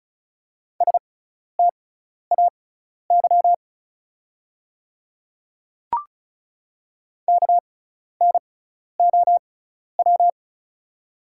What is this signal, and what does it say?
Morse code, heard as a keyed tone, about 36 wpm, spaced as for 12 wpm, 690 Hz. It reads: STAY KNOW
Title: 6x Character Spacing - Top 500 Words - 35wpm